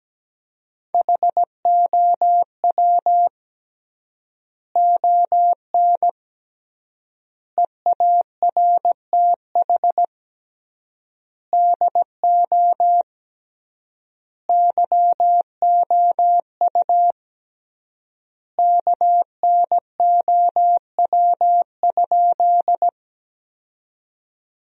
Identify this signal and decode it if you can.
Morse code, heard as a keyed tone, about 17 wpm, 700 Hz. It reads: HOW ON EARTH DO YOU KNOW?
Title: 3x Word-Spacing - Sentences from Top 200 Words - 17wpm